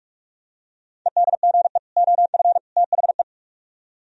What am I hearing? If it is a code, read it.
Morse code, heard as a keyed tone, about 46 wpm, 700 Hz. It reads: EDGE OF THE